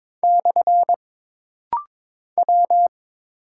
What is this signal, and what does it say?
Morse code, heard as a keyed tone, about 22 wpm, 700 Hz. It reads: / W
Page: This rapid-fire timing is geared to push your ICR (Instant Character Recognition) skills.